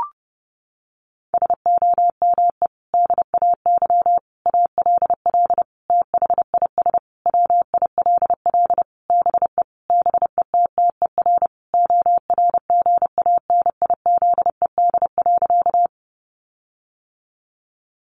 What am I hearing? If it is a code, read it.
Morse code, heard as a keyed tone, about 30 wpm, 700 Hz. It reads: SOME DAY ALL THIS WILL BE BETTER ORGANIZED.
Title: Sentences from Top 1300 Words 30wpm